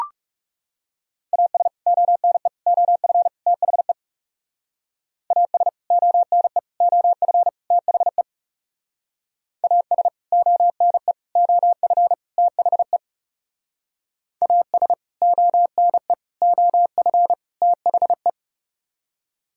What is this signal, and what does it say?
Morse code, heard as a keyed tone, about 35 wpm, 700 Hz. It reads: AS ONE OF THE AS ONE OF THE AS ONE OF THE AS ONE OF THE
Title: Speed-Racing - Sets of 4 Words 30wpm